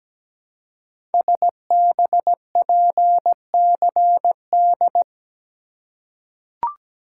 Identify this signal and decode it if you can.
Morse code, heard as a keyed tone, about 17 wpm, 700 Hz. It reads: SBPCD
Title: Code Groups - 5 Letters - 17wpm